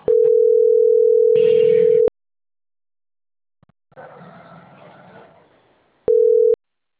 Ringback tone (North America), 440 + 480 Hz.